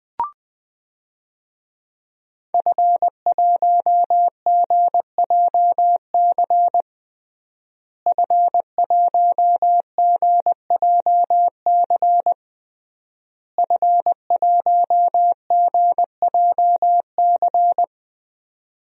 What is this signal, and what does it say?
Morse code, heard as a keyed tone, about 20 wpm, 700 Hz. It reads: F1GJC F1GJC F1GJC
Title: DXCC-100 Call Signs - 5 Characters - Repeated 3x - 20wpm